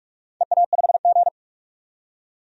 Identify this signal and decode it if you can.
Morse code, heard as a keyed tone, about 44 wpm, 700 Hz. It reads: EA5G
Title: DXCC-100 Call Signs - 4 Characters - 45wpm